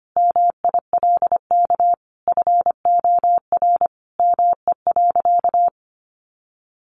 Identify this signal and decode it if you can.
Morse code, heard as a keyed tone, about 25 wpm, 700 Hz. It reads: MILK FOR ME.
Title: Sentences from Top 1500 Words 25wpm